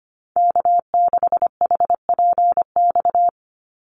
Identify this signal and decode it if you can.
Morse code, heard as a keyed tone, about 25 wpm, 700 Hz. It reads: K6HPX
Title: NA SOTA Chaser Callsigns 25wpm